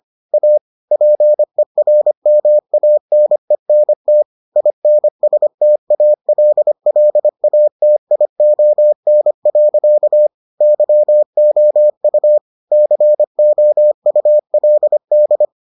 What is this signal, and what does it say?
Morse code, heard as a keyed tone, about 25 wpm, 600 Hz. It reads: A PERMANENT INSTALLATION. YOU COULD